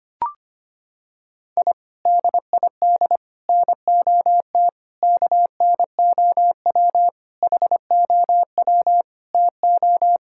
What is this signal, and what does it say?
Morse code, heard as a keyed tone, about 25 wpm, 700 Hz. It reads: I DID NOT KNOW HOW TO
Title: Sentences from Top 1000 Words - Review Part 6 - 25wpm